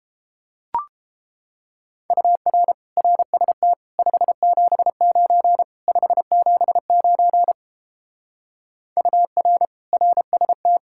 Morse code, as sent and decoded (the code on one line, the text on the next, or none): ..- .-. / .-. ... - / ..... --... ----. / ..... --... ----. / ..- .-. / .-. ... -
UR RST 579 579 UR RST